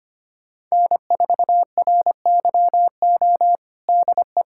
Morse code, heard as a keyed tone, about 25 wpm, 700 Hz. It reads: N4RYO DE
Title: USA Callsign DE Callsign - 5 Characters - 25wpm